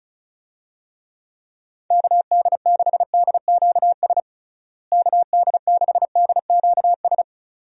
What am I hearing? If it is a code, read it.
Morse code, heard as a keyed tone, about 36 wpm, 700 Hz. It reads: KD6DQS KD6DQS
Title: USA Call Signs - 6 Characters - Repeated 2x - 35wpm